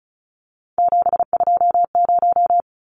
Morse code, heard as a keyed tone, about 34 wpm, 700 Hz. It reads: 720
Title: Numbers - 3-Digit 35wpm